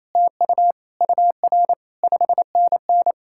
Morse code, heard as a keyed tone, about 37 wpm, 700 Hz. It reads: TU UR 5NN